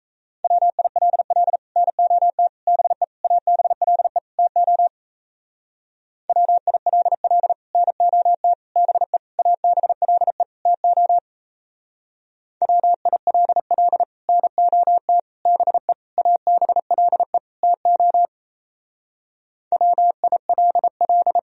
Morse code, heard as a keyed tone, about 36 wpm, 700 Hz. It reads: WILL NOT BE ABLE TO WILL NOT BE ABLE TO WILL NOT BE ABLE TO WILL